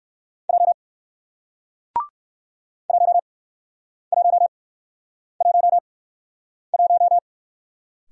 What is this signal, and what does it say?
Morse code, heard as a keyed tone, about 53 wpm, spaced as for 9 wpm, 700 Hz. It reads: J 1111
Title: Speed-Racing Single Letter-Number 45wpm